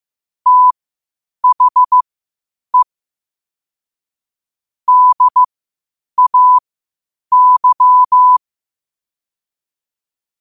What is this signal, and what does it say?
Morse code, heard as a keyed tone, about 15 wpm, spaced as for 8 wpm, 1000 Hz. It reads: THE DAY
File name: qotdfiles / 8wpm